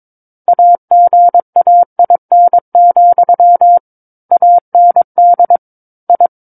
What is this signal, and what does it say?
Morse code, heard as a keyed tone, about 22 wpm, 700 Hz. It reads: AGAIN, AND I
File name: Book One / 13-Chapter 13 - 22 WPM